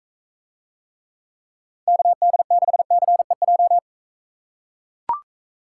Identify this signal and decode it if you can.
Morse code, heard as a keyed tone, about 42 wpm, 700 Hz. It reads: KD6CEJ